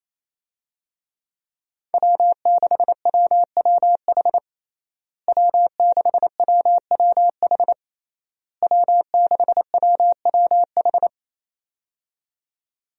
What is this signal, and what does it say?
Morse code, heard as a keyed tone, about 28 wpm, 700 Hz. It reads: W6WWH W6WWH W6WWH
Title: USA Call Signs - 5 Characters - Repeated 3x - 28wpm